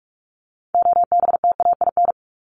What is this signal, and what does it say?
Morse code, heard as a keyed tone, about 44 wpm, 700 Hz. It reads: OBTAIN
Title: Top 1001-1250 Words 45wpm